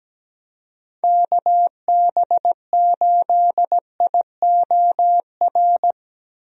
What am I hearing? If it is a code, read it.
Morse code, heard as a keyed tone, about 17 wpm, 700 Hz. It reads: KB8IOR
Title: USA Call Signs - 6 Characters - Repeated 3x - 17wpm